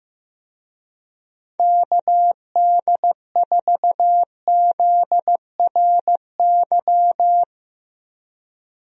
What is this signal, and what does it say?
Morse code, heard as a keyed tone, about 15 wpm, 700 Hz. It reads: KD4ZRY
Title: USA Call Signs - 6 Characters - 15wpm